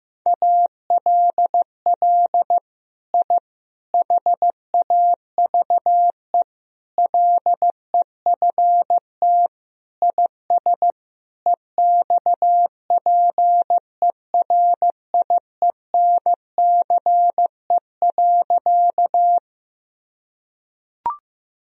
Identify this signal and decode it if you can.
Morse code, heard as a keyed tone, about 15 wpm, 700 Hz. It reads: ALL I HAVE LEFT IS EXPERIENCE.
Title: Sentences from Top 1000 Words - Review Part 3 - 15wpm